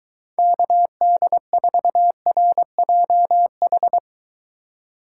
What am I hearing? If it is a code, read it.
Morse code, heard as a keyed tone, about 23 wpm, 700 Hz. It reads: KD4RJH